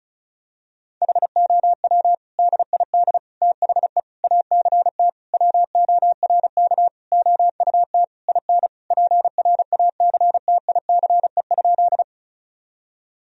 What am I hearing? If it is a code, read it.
Morse code, heard as a keyed tone, about 36 wpm, 700 Hz. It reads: HOW DID THE ACT WORK OUT IN PRACTICE?
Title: Sentences from Top 1000 Words - Review Part 4 - 35wpm